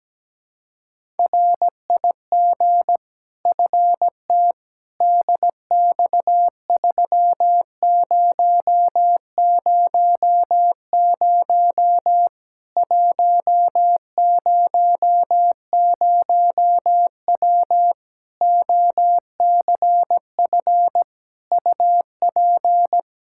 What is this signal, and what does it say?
Morse code, heard as a keyed tone, about 17 wpm, 700 Hz. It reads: RIG FT DX3000 100W OCF UP
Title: K1USN SST QSO Practice 17wpm